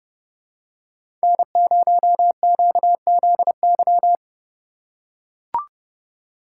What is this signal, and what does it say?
Morse code, heard as a keyed tone, about 30 wpm, 700 Hz. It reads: N0QZY
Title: USA Call Signs - 3x Character Spacing - 30wpm